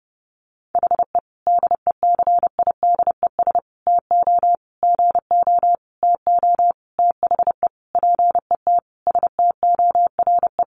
Morse code, heard as a keyed tone, about 30 wpm, 700 Hz. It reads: HE DECIDES TO GO TO THE PET STORE
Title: Additional Easy Phrases 30wpm